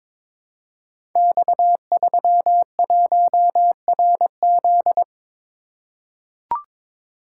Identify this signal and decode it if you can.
Morse code, heard as a keyed tone, about 23 wpm, 700 Hz. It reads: X31RZ
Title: Code Groups - 5 Letter:Numbers - 22wpm